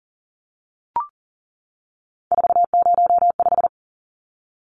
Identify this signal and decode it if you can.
Morse code, heard as a keyed tone, about 40 wpm, 700 Hz. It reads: 405
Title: Numbers - 3-Digit 40wpm